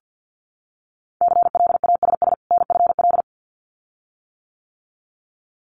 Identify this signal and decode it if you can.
Morse code, heard as a keyed tone, about 50 wpm, 700 Hz. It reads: CLASS NFL
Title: Winter Field Day Exchanges 50wpm